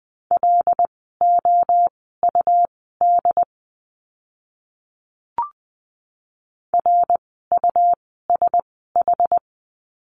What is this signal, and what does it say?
Morse code, heard as a keyed tone, about 20 wpm, spaced as for 15 wpm, 700 Hz. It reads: LOUD RUSH
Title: 101 Lesson 17f - Letter u - Words 20wpm